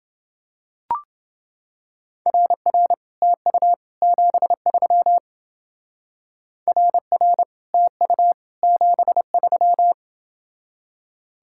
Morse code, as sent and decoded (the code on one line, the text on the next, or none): .-. .-. / - ..- / --... ...-- / .-. .-. / - ..- / --... ...--
RR TU 73 RR TU 73